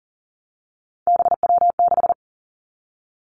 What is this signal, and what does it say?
Morse code, heard as a keyed tone, about 41 wpm, 700 Hz. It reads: BW6
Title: Three Challenging Letter-Number Combinations 40wpm